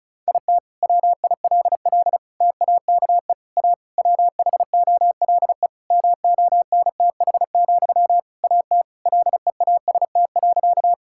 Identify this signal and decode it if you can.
Morse code, heard as a keyed tone, about 36 wpm, 700 Hz. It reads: IT WILL TAKE A WHOLE MONTH, AT LEAST.
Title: Sentences from Top 1000 Words - Review Part 5 - 35wpm